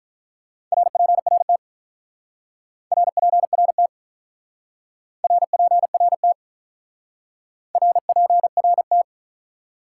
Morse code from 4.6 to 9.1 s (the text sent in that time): RPRT RPRT